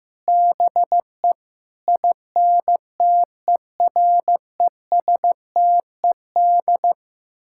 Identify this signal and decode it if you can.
Morse code, heard as a keyed tone, about 15 wpm, 700 Hz. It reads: BE INTERESTED
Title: Sentences from Top 1000 Words - Review Part 6 - 15wpm